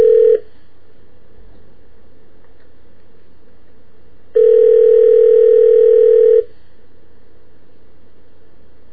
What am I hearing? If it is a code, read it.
Ringback tone (North America), 440 + 480 Hz.